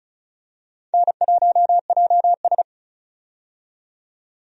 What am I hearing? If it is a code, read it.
Morse code, heard as a keyed tone, about 36 wpm, 700 Hz. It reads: N1JS
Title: USA Call Signs - 4 Characters - 35wpm